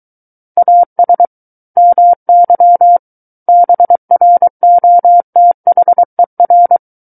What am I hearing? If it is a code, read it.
Morse code, heard as a keyed tone, about 23 wpm, 700 Hz. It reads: AS MY BROTHER
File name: Book One / 14-Chapter 14 - 23 WPM